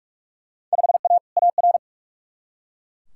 Morse code, heard as a keyed tone, about 42 wpm, 700 Hz. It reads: 5A AR
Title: Field Day Exchanges Encore - 45wpm